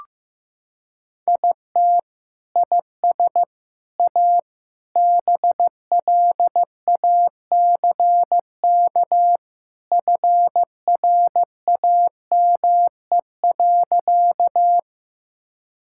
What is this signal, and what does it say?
Morse code, heard as a keyed tone, about 16 wpm, 700 Hz. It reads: IT IS A BLACK FRAME.
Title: Sentences from Top 1000 Words - Review Part 6 - 15wpm